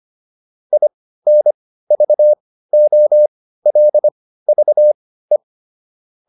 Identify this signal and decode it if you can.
Morse code, heard as a keyed tone, about 25 wpm, spaced as for 15 wpm, 600 Hz. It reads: INVOLVE
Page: intermediate-practice-files